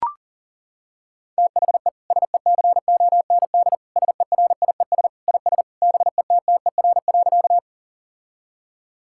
Morse code, heard as a keyed tone, about 40 wpm, 700 Hz. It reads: THE SECOND SERIES IS BETTER.
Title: Sentences from Top 1000 Words - Review Part 5 - 40wpm